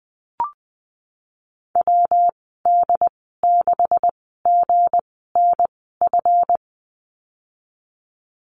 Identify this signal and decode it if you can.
Morse code, heard as a keyed tone, about 20 wpm, spaced as for 14 wpm, 700 Hz. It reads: WD6GNF
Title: 179 Lesson 29g - Slash - Call Signs 20wpm